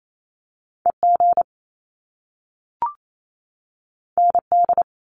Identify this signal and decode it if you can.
Morse code, heard as a keyed tone, about 28 wpm, 700 Hz. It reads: EG ND
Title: Top Two Letter Combinations in Words 28wpm